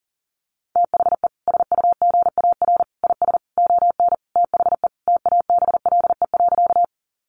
Morse code, heard as a keyed tone, about 40 wpm, 700 Hz. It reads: THE SUGAR IS ON THE TABLE.
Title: Sentences from Top 1200 Words 40wpm